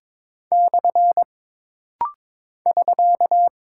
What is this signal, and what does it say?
Morse code, heard as a keyed tone, about 22 wpm, 700 Hz. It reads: / <SK>
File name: practice / 22wpm / Single Punctuation-Prosign - Rapid-Fire - 22wpm